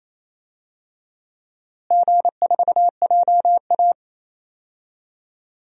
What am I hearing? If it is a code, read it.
Morse code, heard as a keyed tone, about 28 wpm, 700 Hz. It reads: G4JA
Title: DXCC-100 Call Signs - 4 Characters - Repeated 3x - 28wpm